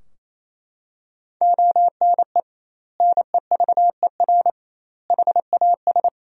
Morse code, heard as a keyed tone, about 28 wpm, 700 Hz. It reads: ONE NEVER HAS
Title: 2x Word-Spacing - Sentences from Top 300 Words - 28wpm